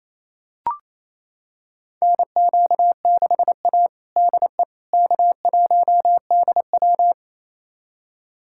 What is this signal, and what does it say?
Morse code, heard as a keyed tone, about 27 wpm, 700 Hz. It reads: NQ6A DE K1DW
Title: USA Callsign DE Callsign - 4 Characters - 28wpm